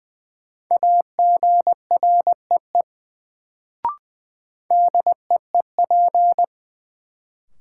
Morse code, heard as a keyed tone, about 20 wpm, 700 Hz. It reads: AGREE DEEP
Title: Sets of 5 Words Encore - New Word Drill - 20wpm